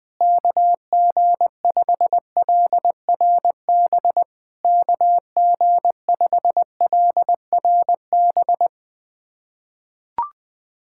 Morse code, heard as a keyed tone, about 20 wpm, 690 Hz. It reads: KG5LRB KG5LRB
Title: POTA QSO Practice - 20wpm